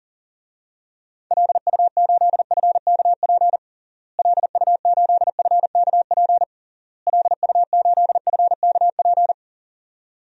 Morse code, heard as a keyed tone, about 40 wpm, 700 Hz. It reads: LU8FKP LU8FKP LU8FKP
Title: DXCC-100 Call Signs - 6 Characters - Repeated 3x - 40wpm